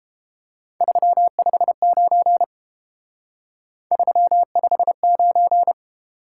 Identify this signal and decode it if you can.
Morse code, heard as a keyed tone, about 28 wpm, 700 Hz. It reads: 359 359